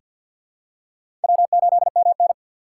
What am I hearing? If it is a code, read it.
Morse code, heard as a keyed tone, about 47 wpm, 700 Hz. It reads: W8MN
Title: USA Call Signs - 4 Characters - 2x Character Spacing - 50wpm